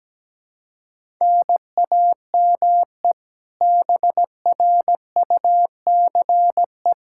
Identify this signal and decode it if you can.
Morse code, heard as a keyed tone, about 17 wpm, 700 Hz. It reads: NAME BRUCE
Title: Practice QSO - Exchange 1 - Beginning - 17wpm